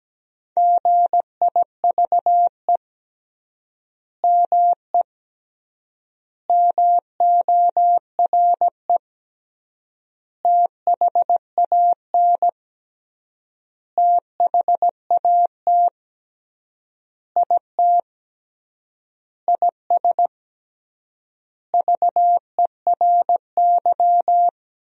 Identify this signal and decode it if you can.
Morse code, heard as a keyed tone, about 17 wpm, 700 Hz. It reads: GIVE ME MORE THAN THAT IT IS VERY
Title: 3x Word-Spacing - Sentences from Top 200 Words - 17wpm